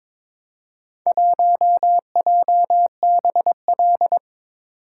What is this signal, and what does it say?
Morse code, heard as a keyed tone, about 22 wpm, 700 Hz. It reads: 1JBL
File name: practice / 22wpm / Four Letter-Number Combinations - 22wpm